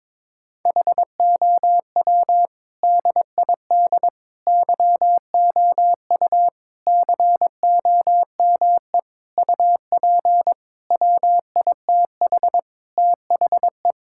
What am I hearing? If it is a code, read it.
Morse code, heard as a keyed tone, about 22 wpm, 700 Hz. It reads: HOW DID YOU COME UP WITH THE
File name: practice / 22wpm / Sentences with 8-Word Phrases - 22wpm